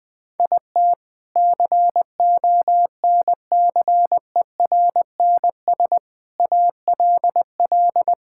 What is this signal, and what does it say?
Morse code, heard as a keyed tone, about 20 wpm, 700 Hz. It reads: IT CONCERNS ALL
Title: Sentences with 3-Word Phrases - Spoken Beginning - 20wpm